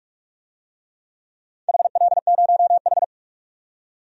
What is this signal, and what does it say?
Morse code, heard as a keyed tone, about 46 wpm, 700 Hz. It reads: HL0H
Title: DXCC-100 Call Signs - 4 Characters - 45wpm